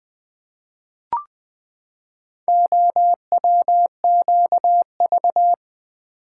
Morse code, heard as a keyed tone, about 20 wpm, 700 Hz. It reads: OWQV
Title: Four Letter-Number Combinations - 20wpm